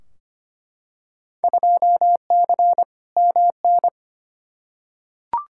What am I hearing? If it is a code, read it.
Morse code, heard as a keyed tone, about 25 wpm, 700 Hz. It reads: 2C MN
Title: Field Day Exchanges Encore - 25wpm